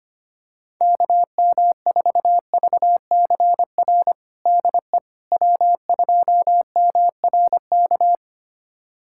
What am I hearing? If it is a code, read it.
Morse code, heard as a keyed tone, about 25 wpm, 700 Hz. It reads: KM4VCR DE W2MRK
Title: USA Callsign DE Callsign 25wpm